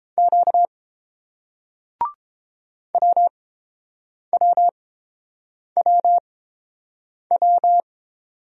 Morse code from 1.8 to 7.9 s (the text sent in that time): WWWW